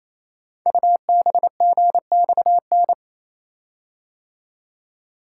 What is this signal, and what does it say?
Morse code, heard as a keyed tone, about 27 wpm, 700 Hz. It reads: UBGXN
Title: Code Groups - 5 Letters - 28wpm